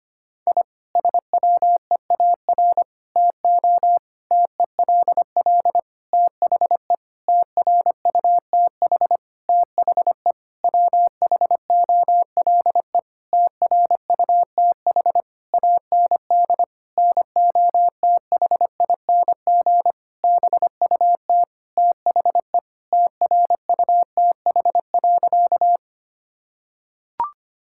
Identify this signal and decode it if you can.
Morse code, heard as a keyed tone, about 25 wpm, 700 Hz. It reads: I SWEAR TO TELL THE TRUTH THE WHOLE TRUTH AND NOTHING BUT THE TRUTH.
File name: practice / 25wpm / Sentences with 9-Word Phrases - 25wpm